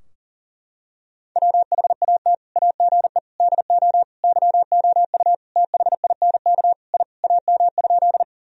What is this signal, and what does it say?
Morse code, heard as a keyed tone, about 40 wpm, 700 Hz. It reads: WHAT AGE DO YOU THINK I AM?
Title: Sentences from Top 1000 Words - Review Part 3 - 40wpm